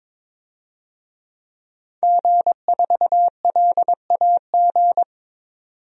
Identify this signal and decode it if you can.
Morse code, heard as a keyed tone, about 22 wpm, 700 Hz. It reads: G4LAG
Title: DXCC-100 Call Signs - 5 Characters - 22wpm